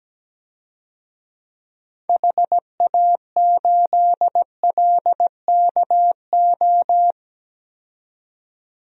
Morse code, heard as a keyed tone, about 17 wpm, 700 Hz. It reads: HA8LKO
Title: DXCC-100 Call Signs - 6 Characters - 17wpm